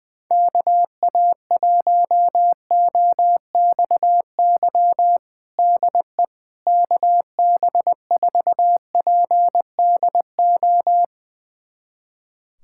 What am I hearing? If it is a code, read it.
Morse code, heard as a keyed tone, about 20 wpm, 700 Hz. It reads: KA1OXY DE KB4PDO
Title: USA Callsign DE Callsign - 6 Characters - 20wpm